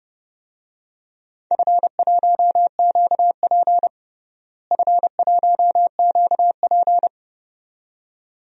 Morse code, heard as a keyed tone, about 30 wpm, 700 Hz. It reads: F1QP F1QP
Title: DXCC-100 Call Signs - 4 Characters - Repeated 2x - 30wpm